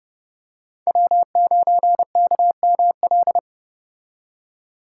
Morse code, heard as a keyed tone, about 30 wpm, 700 Hz. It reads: W9KML